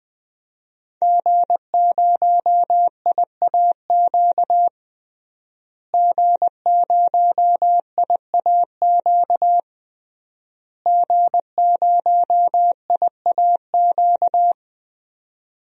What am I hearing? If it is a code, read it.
Morse code, heard as a keyed tone, about 21 wpm, 700 Hz. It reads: G0IAQ G0IAQ G0IAQ
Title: DXCC-100 Call Signs - 5 Characters - Repeated 3x - 20wpm